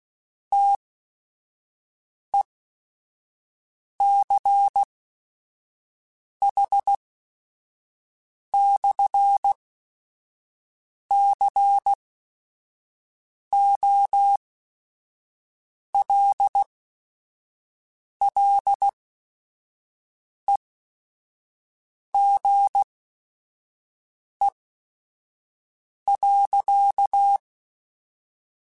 Morse code, heard as a keyed tone, about 16 wpm, spaced as for 5 wpm, 800 Hz. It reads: TECH/COLLEGE.
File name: QSOs / QSOs-mp3s / WD0BGP 1@5